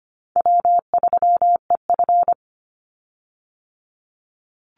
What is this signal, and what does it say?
Morse code, heard as a keyed tone, about 25 wpm, 700 Hz. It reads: W3EF
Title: Easy Call Signs Repeated 3x - 25wpm